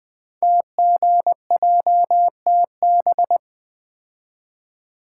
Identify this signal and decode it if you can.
Morse code, heard as a keyed tone, about 20 wpm, 700 Hz. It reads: TGJTB
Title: Code Groups - 5 Letters - 20wpm